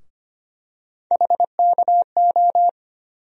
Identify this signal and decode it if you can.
Morse code, heard as a keyed tone, about 25 wpm, 700 Hz. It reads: HKO
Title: USA Call Sign Postfixes - 3 Letters 25wpm